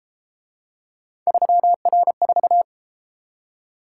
Morse code, heard as a keyed tone, about 33 wpm, 700 Hz. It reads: 3R4